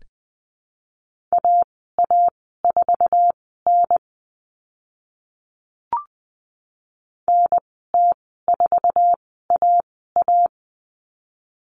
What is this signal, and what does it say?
Morse code, heard as a keyed tone, about 20 wpm, spaced as for 15 wpm, 700 Hz. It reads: AA4N NT4AA
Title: 062 Lesson 11g - Letter h - Call Signs 20wpm